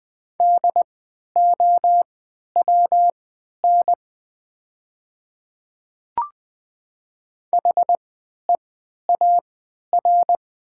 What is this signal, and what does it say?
Morse code, heard as a keyed tone, about 20 wpm, spaced as for 11 wpm, 700 Hz. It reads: DOWN HEAR
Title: 3x Character Spacing - Top 500 Words - 20wpm